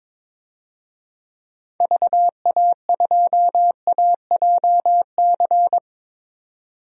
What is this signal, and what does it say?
Morse code, heard as a keyed tone, about 22 wpm, 700 Hz. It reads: VA2AJC
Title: DXCC-100 Call Signs - 6 Characters - 22wpm